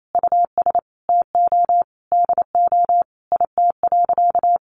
Morse code, heard as a keyed tone, about 28 wpm, 700 Hz. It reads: US TO DO IT.
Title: Sentences from Top 1500 Words 28wpm